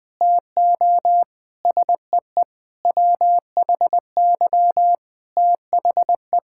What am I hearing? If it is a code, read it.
Morse code, heard as a keyed tone, about 20 wpm, 700 Hz. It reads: TO SEE WHY THE
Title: Sentences with 6-Word Phrases - 20wpm